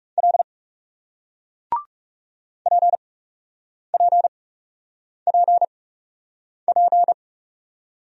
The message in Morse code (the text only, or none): L PPPP